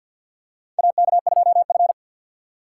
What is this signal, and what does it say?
Morse code, heard as a keyed tone, about 52 wpm, 700 Hz. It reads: AK2F
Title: USA Call Signs - 4 Characters - 3x Character Spacing - 50wpm